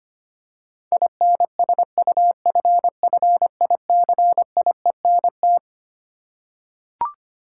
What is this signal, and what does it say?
Morse code, heard as a keyed tone, about 25 wpm, 700 Hz. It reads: INSUFFICIENT
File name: Twelve Letter Words - 25wpm